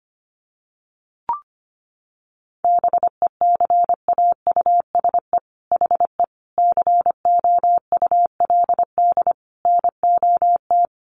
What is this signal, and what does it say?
Morse code, heard as a keyed tone, about 25 wpm, 700 Hz. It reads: BECAUSE HE COULD NOT
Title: Sentences from Top 1400 Words 25wpm